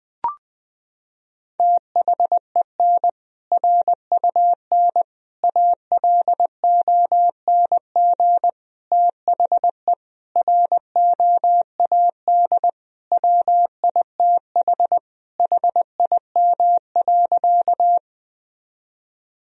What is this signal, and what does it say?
Morse code, heard as a keyed tone, about 20 wpm, 700 Hz. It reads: THEN RUN ALONG THE ROAD WITH HIM.
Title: Sentences from Top 1000 Words - Review Part 2 - 20wpm